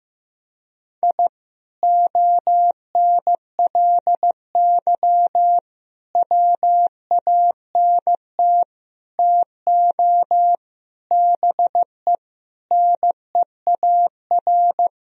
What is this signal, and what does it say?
Morse code, heard as a keyed tone, about 15 wpm, 700 Hz. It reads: I ONLY WANT TO BE NEAR
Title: Sentences from Top 1000 Words - Review Part 1 - 15wpm